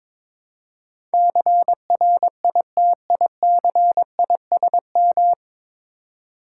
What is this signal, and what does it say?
Morse code, heard as a keyed tone, about 21 wpm, 700 Hz. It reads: CRITICISM
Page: Morse Code Ninja practice 22wpm